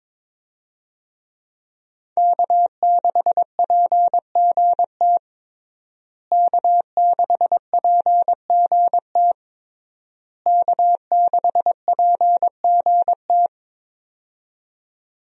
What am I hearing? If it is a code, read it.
Morse code, heard as a keyed tone, about 21 wpm, 700 Hz. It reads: K6PGT K6PGT K6PGT
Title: USA Call Signs - 5 Characters - Repeated 3x - 22wpm